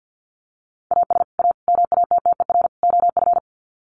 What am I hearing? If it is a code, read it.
Morse code, heard as a keyed tone, about 50 wpm, 700 Hz. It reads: AS A MATTER OF
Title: Sets of 4 Words 50wpm